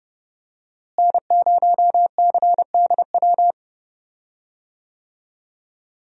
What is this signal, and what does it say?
Morse code, heard as a keyed tone, about 30 wpm, 700 Hz. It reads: N0CDW
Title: USA Call Signs - 5 Characters - Repeated 2x - 30wpm